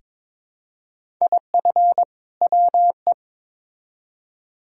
Morse code, heard as a keyed tone, about 22 wpm, 700 Hz. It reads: IF WE
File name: Build A Phrase - 2-Word Phrases - 22wpm